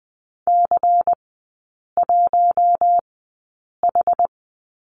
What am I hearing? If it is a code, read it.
Morse code, heard as a keyed tone, about 20 wpm, spaced as for 8 wpm, 700 Hz. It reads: C1H
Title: 166 Lesson 27e - Number 7 - Three Characters 20wpm